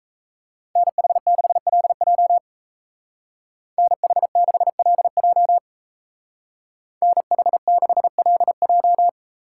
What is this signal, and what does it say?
Morse code, heard as a keyed tone, about 34 wpm, 700 Hz. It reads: NH6LJ NH6LJ NH6LJ
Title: Speed Racing - USA 2x2 Call Signs 28wpm